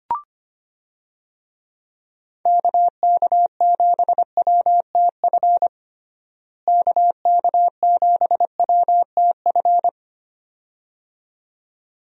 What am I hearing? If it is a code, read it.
Morse code, heard as a keyed tone, about 25 wpm, 700 Hz. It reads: KK7WTF KK7WTF
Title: USA Call Signs - 6 Characters - Repeated 2x - 25wpm